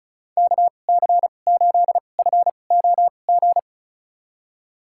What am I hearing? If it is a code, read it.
Morse code, heard as a keyed tone, about 36 wpm, spaced as for 25 wpm, 700 Hz. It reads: KC8FOG
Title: USA Call Signs - 2x Character Spacing - 35wpm